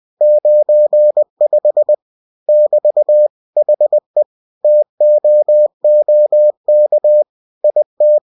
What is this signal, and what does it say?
Morse code, heard as a keyed tone, about 20 wpm, 600 Hz. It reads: 95 <BT> HE TOOK IT